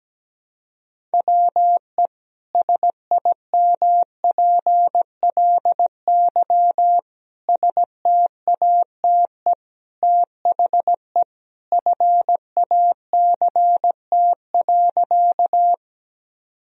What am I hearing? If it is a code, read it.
Morse code, heard as a keyed tone, about 17 wpm, 700 Hz. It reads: WE SIMPLY STATE THE FACT.
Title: Sentences from Top 1000 Words - Review Part 4 - 17wpm